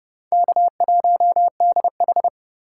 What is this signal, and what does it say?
Morse code, heard as a keyed tone, about 29 wpm, 700 Hz. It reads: K1DH
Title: USA Call Signs - 4 Characters - 2x Character Spacing - 30wpm